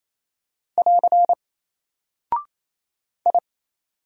Morse code, heard as a keyed tone, about 28 wpm, 700 Hz. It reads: <AR> I
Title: Single Character (Letter:Number:Punctuation:Prosign) Alt 28wpm